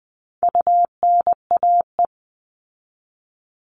Morse code, heard as a keyed tone, about 20 wpm, 700 Hz. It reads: UNAE
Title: All Cut Numbers - 4-Digit 20wpm